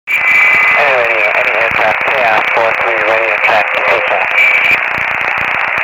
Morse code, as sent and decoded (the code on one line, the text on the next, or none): none